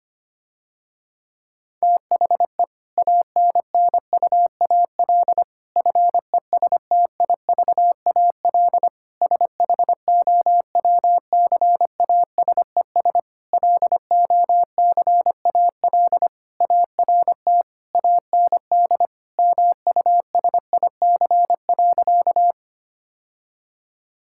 Morse code, heard as a keyed tone, about 25 wpm, 700 Hz. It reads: THE ANNUAL FESTIVAL SHOWCASES LOCAL ART AND MUSIC.
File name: practice / 25wpm / Difficult Sentences with Hints - 25wpm